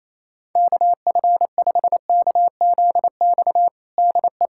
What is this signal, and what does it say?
Morse code, heard as a keyed tone, about 28 wpm, 700 Hz. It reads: KF5KZX DE
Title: USA Callsign DE Callsign 28wpm